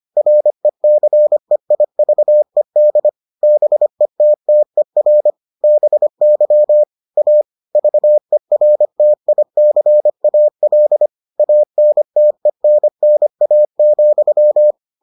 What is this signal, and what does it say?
Morse code, heard as a keyed tone, about 25 wpm, 600 Hz. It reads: RECEIVED BETTER BY A VERTICAL ANTENNA,